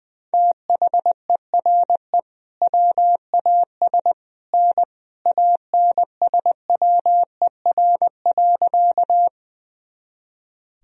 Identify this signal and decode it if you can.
Morse code, heard as a keyed tone, about 21 wpm, 700 Hz. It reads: THERE WAS N ANSWER.
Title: Sentences from Top 300 Words and Abbreviations 20wpm